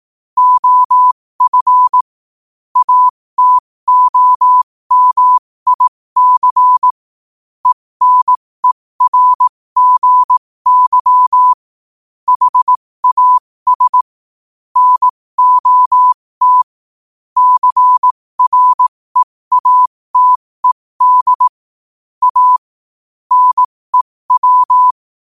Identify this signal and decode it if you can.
Morse code, heard as a keyed tone, about 18 wpm, spaced as for 15 wpm, 1000 Hz. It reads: OF ATOMIC ENERGY HAS NOT CREATED A NEW
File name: qotdfiles / 15wpm